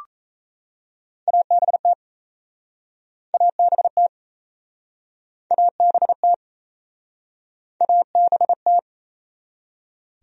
Morse code, heard as a keyed tone, about 38 wpm, 700 Hz. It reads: ABT ABT ABT ABT